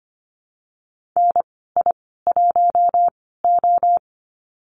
Morse code, heard as a keyed tone, about 25 wpm, spaced as for 16 wpm, 700 Hz. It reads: NI1O
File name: Lesson 15g - Number 5 - Call Signs 25wpm